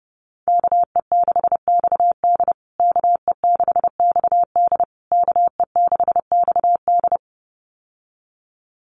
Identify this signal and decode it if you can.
Morse code, heard as a keyed tone, about 30 wpm, 700 Hz. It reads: KE6XD KE6XD KE6XD
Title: USA Call Signs Repeated 3x 30wpm